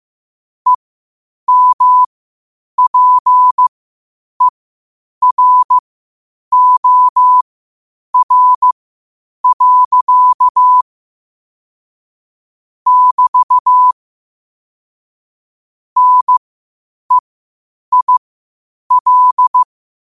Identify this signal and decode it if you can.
Morse code, heard as a keyed tone, about 15 wpm, spaced as for 8 wpm, 1000 Hz. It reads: EMPEROR. <BT> NEIL
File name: qotdfiles / 8wpm